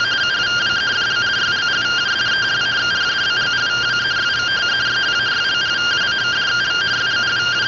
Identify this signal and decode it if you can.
Two-tone teleprinter signal, about 170 Hz shift.
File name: findings / MR CORONAL 55.gen-17 rtty rx